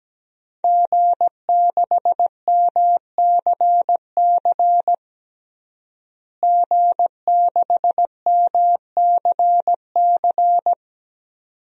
Morse code, heard as a keyed tone, about 17 wpm, 700 Hz. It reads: G6MCC G6MCC
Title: DXCC-100 Call Signs - 5 Characters - Repeated 2x - 17wpm